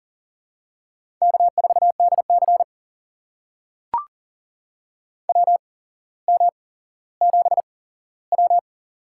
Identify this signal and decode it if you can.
Morse code, heard as a keyed tone, about 40 wpm, spaced as for 17 wpm, 700 Hz. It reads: K4DC WM7W
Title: USA Call Signs - 4 Characters - 8x Character Spacing - 40wpm